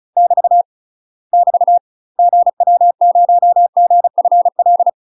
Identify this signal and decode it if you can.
Morse code, heard as a keyed tone, about 34 wpm, 700 Hz. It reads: <BT> <BT> GW0GFL
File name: UK MASTER QSOs TRAINING / 35WPM / UK MASTER QSOs 0030